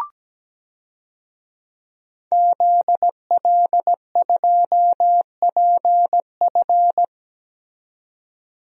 Morse code, heard as a keyed tone, about 17 wpm, 700 Hz. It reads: ZL2PF
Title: DXCC-100 Call Signs - 5 Characters - 17wpm